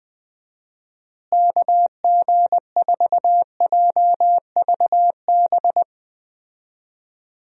Morse code, heard as a keyed tone, about 21 wpm, 700 Hz. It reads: KG4JVB
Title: USA Call Signs - 6 Characters - Repeated 2x - 20wpm